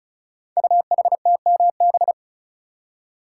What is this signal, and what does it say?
Morse code, heard as a keyed tone, about 36 wpm, 700 Hz. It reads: UHTMB
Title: Code Groups - 5 Letter:Numbers - 35wpm